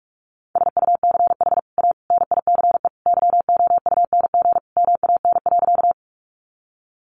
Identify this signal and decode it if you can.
Morse code, heard as a keyed tone, about 44 wpm, 700 Hz. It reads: SUCH A NICE YOUNG MAN.
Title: Sentences from Top 1100 Words 45wpm